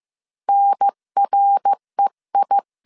Morse code, heard as a keyed tone, about 15 wpm, 800 Hz. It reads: NREI
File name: CW / DET / 1 3 1500C S A 14wpm 2447